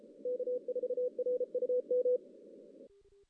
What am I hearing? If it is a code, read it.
Morse code, heard as a keyed tone, about 33 wpm, 500 Hz. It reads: K4RUM
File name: 2016 JAN CW NAQP / 2016 Jan naqp k5ot - 0-6